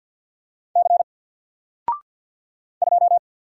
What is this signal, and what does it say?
Morse code, heard as a keyed tone, about 48 wpm, 700 Hz. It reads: C 2
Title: Single Letter-Number Alt 50wpm